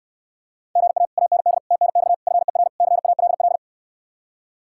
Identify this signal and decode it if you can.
Morse code, heard as a keyed tone, about 69 wpm, 700 Hz. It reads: GA AND TNX FER CALL